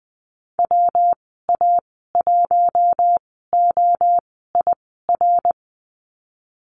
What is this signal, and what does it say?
Morse code, heard as a keyed tone, about 20 wpm, spaced as for 14 wpm, 700 Hz. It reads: WA1OIR
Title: 140 Lesson 23g - Letter f - Call Signs 20wpm